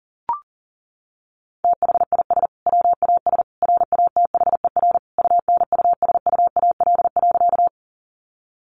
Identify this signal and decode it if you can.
Morse code, heard as a keyed tone, about 40 wpm, 700 Hz. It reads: THIS WAS RATHER UNUSUAL.
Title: Sentences from Top 1400 Words 40wpm